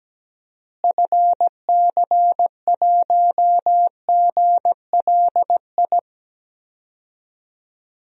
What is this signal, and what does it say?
Morse code, heard as a keyed tone, about 17 wpm, 700 Hz. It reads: FC1GLI
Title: DXCC-100 Call Signs - 6 Characters - Repeated 2x - 17wpm